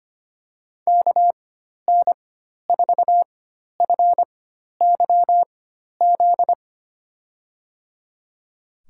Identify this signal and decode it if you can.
Morse code, heard as a keyed tone, about 25 wpm, spaced as for 12 wpm, 700 Hz. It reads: KN4FYZ
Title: USA Call Signs - 4x Character Spacing - 25wpm